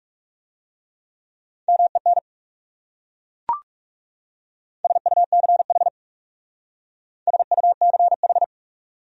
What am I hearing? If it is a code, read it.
Morse code, heard as a keyed tone, about 38 wpm, 700 Hz. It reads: MEN SUCH SUCH